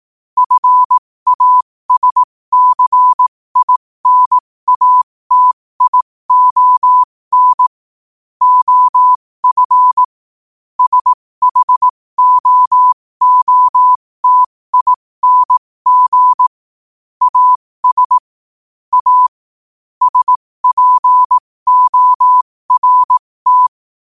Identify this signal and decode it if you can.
Morse code, heard as a keyed tone, about 18 wpm, spaced as for 15 wpm, 1000 Hz. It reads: FASCINATION OF SHOOTING AS A SPORT
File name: qotdfiles / 15wpm